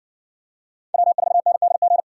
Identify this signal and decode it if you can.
Morse code, heard as a keyed tone, about 60 wpm, 700 Hz. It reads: W4NDG